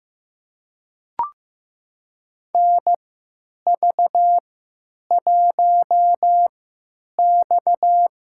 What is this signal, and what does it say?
Morse code, heard as a keyed tone, about 15 wpm, spaced as for 9 wpm, 700 Hz. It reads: NV1X
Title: USA Call Signs - 4 Characters - 3x Character Spacing - 15wpm